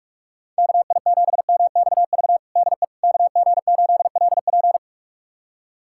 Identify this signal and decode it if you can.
Morse code, heard as a keyed tone, about 46 wpm, 700 Hz. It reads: KI7MXV DE KG8LP
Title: USA Callsign DE Callsign 45wpm